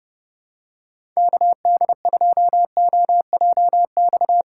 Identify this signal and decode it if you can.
Morse code, heard as a keyed tone, about 30 wpm, 700 Hz. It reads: KD2OJX
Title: USA Call Signs - 6 Characters - 3x Character Spacing - 30wpm